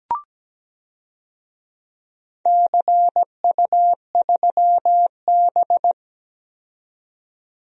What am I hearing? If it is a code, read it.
Morse code, heard as a keyed tone, about 17 wpm, 700 Hz. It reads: CU3B
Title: DXCC-100 Call Signs - 4 Characters - 17wpm